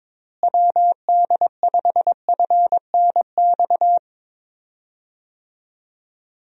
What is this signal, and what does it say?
Morse code, heard as a keyed tone, about 22 wpm, 700 Hz. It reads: WD5FNX
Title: USA Call Signs - 6 Characters - Repeated 2x - 22wpm